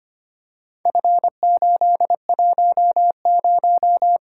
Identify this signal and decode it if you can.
Morse code, heard as a keyed tone, about 25 wpm, 700 Hz. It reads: F810
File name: practice / 25wpm / Four Letter-Number Combinations - 25wpm